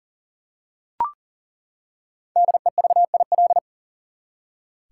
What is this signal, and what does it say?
Morse code, heard as a keyed tone, about 40 wpm, 700 Hz. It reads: DEVIL
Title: Five Letter Words - 40wpm